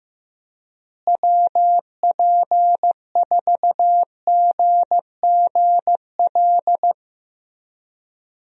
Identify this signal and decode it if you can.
Morse code, heard as a keyed tone, about 15 wpm, 700 Hz. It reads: WP4GGL
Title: USA Call Signs - 6 Characters - Repeated 3x - 15wpm